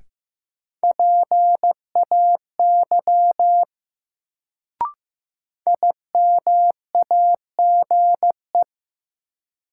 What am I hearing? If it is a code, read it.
Morse code, heard as a keyed tone, about 15 wpm, 700 Hz. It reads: PAY IMAGE
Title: Sentences from Top 700 Words - New Word Drill - 15wpm